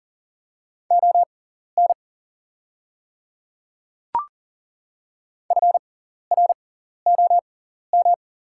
Morse code, heard as a keyed tone, about 40 wpm, spaced as for 14 wpm, 700 Hz. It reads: ON FROM
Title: 6x Character Spacing - Top 200 Words - 40wpm